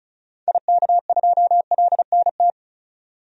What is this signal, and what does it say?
Morse code, heard as a keyed tone, about 36 wpm, 700 Hz. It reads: IK2LNT
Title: DXCC-100 Call Signs - 6 Characters - 35wpm